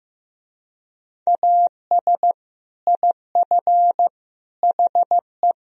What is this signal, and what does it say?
Morse code, heard as a keyed tone, about 15 wpm, 700 Hz. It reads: AS IF HE
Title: Speed-Racing - Sets of 4 Words 15wpm